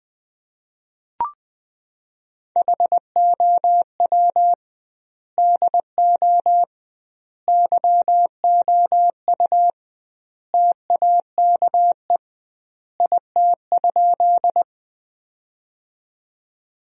Morse code, heard as a keyed tone, about 20 wpm, 700 Hz. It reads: HOW DO YOU TAKE IT?
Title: 2x Word-Spacing - Sentences from Top 200 Words - 20wpm